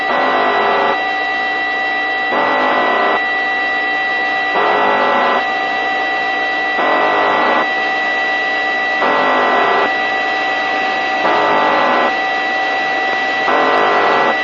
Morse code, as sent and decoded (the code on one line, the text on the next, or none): none